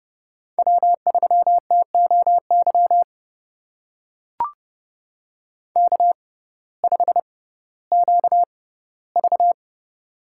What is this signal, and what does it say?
Morse code, heard as a keyed tone, about 31 wpm, spaced as for 17 wpm, 700 Hz. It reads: W3TOY K5QV
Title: USA Call Signs - 6x Character Spacing - 30wpm